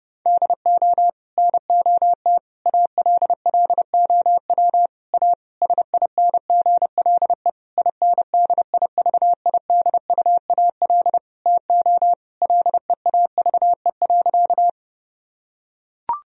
Morse code, heard as a keyed tone, about 30 wpm, 700 Hz. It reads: DO NOT ALLOW A SINGLE INDIVIDUAL TO LEAVE.
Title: Sentences from Top 1000 Words - Review Part 4 - 30wpm